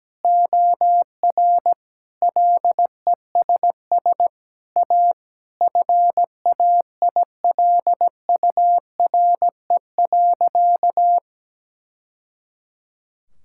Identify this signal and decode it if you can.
Morse code, heard as a keyed tone, about 17 wpm, 700 Hz. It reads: OR LESS A FAILURE.
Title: Sentences from Top 1000 Words - Review Part 6 - 17wpm